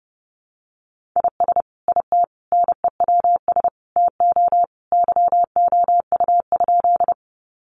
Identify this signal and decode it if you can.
Morse code, heard as a keyed tone, about 30 wpm, 700 Hz. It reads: IS IT NEWS TO YOU?
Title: Sentences from Top 800 Words 30wpm